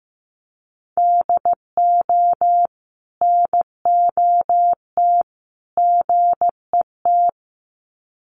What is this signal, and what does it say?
Morse code, heard as a keyed tone, about 15 wpm, 700 Hz. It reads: DO NOT GET
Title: Sets of 3 Words 15wpm